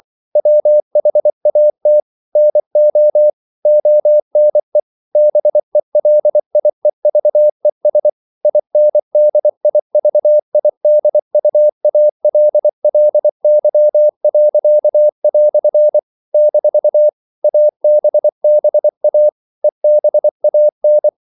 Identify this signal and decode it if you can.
Morse code, heard as a keyed tone, about 24 wpm, 600 Hz. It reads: WHAT NO ONE BELIEVES INDIVIDUALLY." - ABBA EBAN
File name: CW / Practice / EnglishQuotations0424